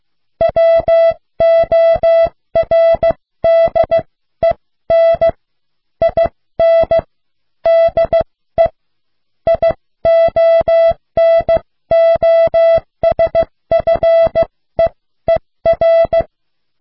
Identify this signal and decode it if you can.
Morse code, heard as a keyed tone, about 15 wpm, spaced as for 13 wpm, 660 Hz. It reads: WORDEN IN DE IONOSFEER